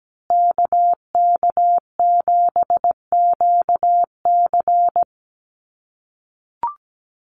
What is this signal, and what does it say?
Morse code, heard as a keyed tone, about 17 wpm, 700 Hz. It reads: KK7QC
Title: USA Call Signs Repeated 2x 17wpm